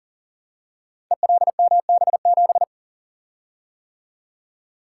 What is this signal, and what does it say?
Morse code, heard as a keyed tone, about 40 wpm, 700 Hz. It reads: ELMB7
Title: Code Groups - 5 Letter:Numbers - 40wpm